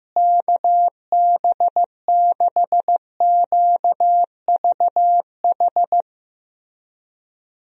Morse code, heard as a keyed tone, about 15 wpm, 700 Hz. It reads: KB6QVH
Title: USA Call Signs - 6 Characters - Repeated 3x - 15wpm